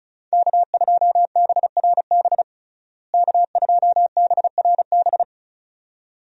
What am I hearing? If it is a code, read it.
Morse code, heard as a keyed tone, about 36 wpm, 700 Hz. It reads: K2BRB K2BRB
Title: USA Call Signs - 5 Characters - Repeated 2x - 35wpm